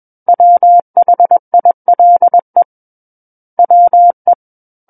Morse code, heard as a keyed tone, about 21 wpm, 700 Hz. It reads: WHILE WE
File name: Book One / 12-Chapter 12 - 21 WPM